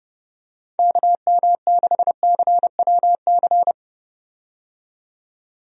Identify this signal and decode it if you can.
Morse code, heard as a keyed tone, about 30 wpm, 700 Hz. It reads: KM6CWC
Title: USA Call Signs - 6 Characters - 30wpm